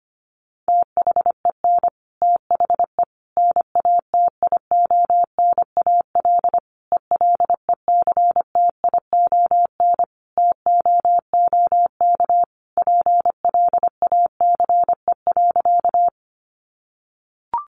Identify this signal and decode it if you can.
Morse code, heard as a keyed tone, about 25 wpm, 700 Hz. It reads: THEN THE NATIONAL ELECTION TOOK PLACE.
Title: Sentences from Top 1500 Words 25wpm